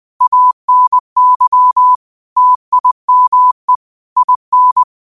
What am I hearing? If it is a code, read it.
Morse code, heard as a keyed tone, about 20 wpm, 1000 Hz. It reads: ANY TIME IN